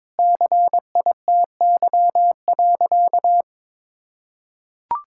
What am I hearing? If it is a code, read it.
Morse code, heard as a keyed tone, about 22 wpm, 700 Hz. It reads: CITY.
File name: practice / 22wpm / Sentences with 3-Word Phrases - 22wpm